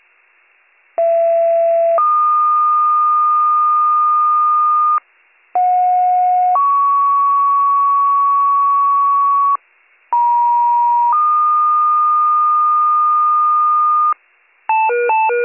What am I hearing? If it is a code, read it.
Two-tone page, heard as a one-second tone, then a three-second tone, about 670 then 1220 Hz, about 730 then 1120 Hz, about 950 then 1220 Hz. It is heard from 1.0 to 14.1 s.